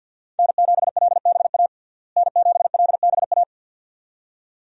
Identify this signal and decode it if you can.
Morse code, heard as a keyed tone, about 49 wpm, 700 Hz. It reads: N7LBA N7LBA
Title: USA Call Signs - 5 Characters - Repeated 2x - 50wpm